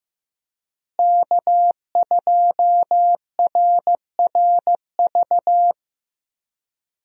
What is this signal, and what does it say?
Morse code, heard as a keyed tone, about 16 wpm, 700 Hz. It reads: K2RRV
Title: USA Call Signs - 5 Characters - Repeated 3x - 15wpm